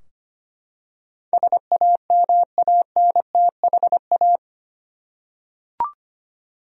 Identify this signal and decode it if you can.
Morse code, heard as a keyed tone, about 25 wpm, 700 Hz. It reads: SAMANTHA
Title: USA Common Names - 25wpm